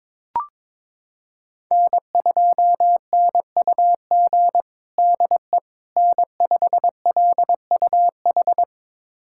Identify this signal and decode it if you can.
Morse code, heard as a keyed tone, about 22 wpm, 700 Hz. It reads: N2NUG DE N5LUH
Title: USA Callsign DE Callsign - 5 Characters - 22wpm